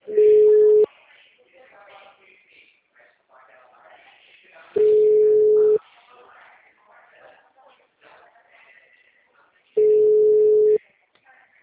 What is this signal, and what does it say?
Ringback tone (Europe, 425 Hz), with speech.